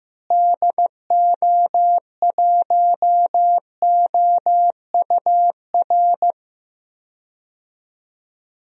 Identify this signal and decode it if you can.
Morse code, heard as a keyed tone, about 16 wpm, 700 Hz. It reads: DO1OUR
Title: DXCC-100 Call Signs - 6 Characters - Repeated 2x - 15wpm